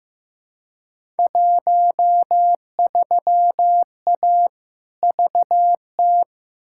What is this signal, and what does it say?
Morse code, heard as a keyed tone, about 15 wpm, 700 Hz. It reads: 13A VT
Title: Field Day Exchanges Encore - 15wpm